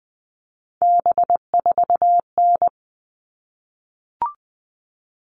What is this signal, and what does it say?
Morse code, heard as a keyed tone, about 20 wpm, 700 Hz. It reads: B4N
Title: All Cut Numbers - 3-Digit 20wpm